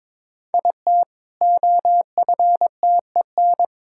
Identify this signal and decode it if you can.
Morse code, heard as a keyed tone, about 22 wpm, 700 Hz. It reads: IT OFTEN
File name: practice / 22wpm / Easy Sentences with Hints - 22wpm